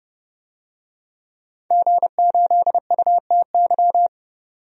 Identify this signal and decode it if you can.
Morse code, heard as a keyed tone, about 32 wpm, 700 Hz. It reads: G8UTY
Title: DXCC-100 Call Signs - 5 Characters - 30wpm